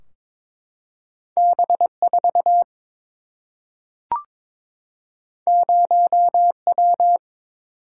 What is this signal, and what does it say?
Morse code, heard as a keyed tone, about 23 wpm, 700 Hz. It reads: B 4 0 W
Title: Single-Letter Single-Number 22wpm